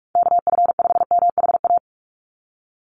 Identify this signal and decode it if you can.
Morse code, heard as a keyed tone, about 44 wpm, 700 Hz. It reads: KF5MHA
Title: USA Call Signs 45wpm